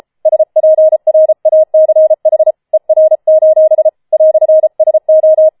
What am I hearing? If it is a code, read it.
Morse code, heard as a keyed tone, about 33 wpm, 630 Hz. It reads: SPRACH ER: "SO
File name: Schneewittchen Teil5 34wpm